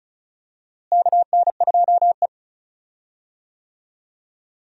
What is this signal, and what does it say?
Morse code, heard as a keyed tone, about 34 wpm, 700 Hz. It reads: KN2E
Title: USA Call Signs - 4 Characters - Repeated 3x - 35wpm